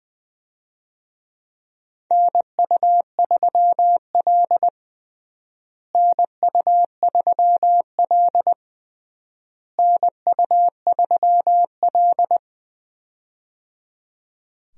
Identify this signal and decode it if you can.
Morse code, heard as a keyed tone, about 20 wpm, 700 Hz. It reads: NU3L NU3L NU3L
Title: USA Call Signs - 4 Characters - Repeated 3x - 20wpm